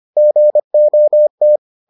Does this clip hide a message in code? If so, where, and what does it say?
Yes: Morse code from 0.2 to 1.6 s, heard as a keyed tone, about 25 wpm, 600 Hz. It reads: GOT